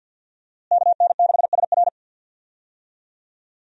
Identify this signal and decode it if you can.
Morse code, heard as a keyed tone, about 46 wpm, 700 Hz. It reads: KN6SR
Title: USA Call Signs - 5 Characters - Repeated 2x - 50wpm